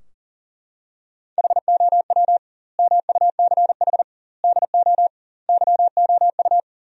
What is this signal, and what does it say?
Morse code, heard as a keyed tone, about 40 wpm, 700 Hz. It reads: HOW MUCH DO YOU
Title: 2x Word-Spacing - Sentences from Top 200 Words - 40wpm